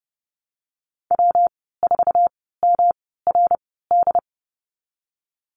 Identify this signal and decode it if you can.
Morse code, heard as a keyed tone, about 30 wpm, spaced as for 17 wpm, 700 Hz. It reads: W4MRD
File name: Lesson 19g - Letter w - Call Signs 30wpm